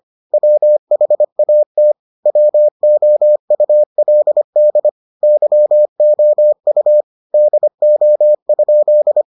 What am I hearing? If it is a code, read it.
Morse code, heard as a keyed tone, about 25 wpm, 600 Hz. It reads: WHAT WOULD YOU DO?